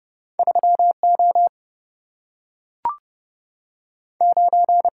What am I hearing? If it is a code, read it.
Morse code, heard as a keyed tone, about 30 wpm, 700 Hz. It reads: 3 O 9
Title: Single-Letter Single-Number 30wpm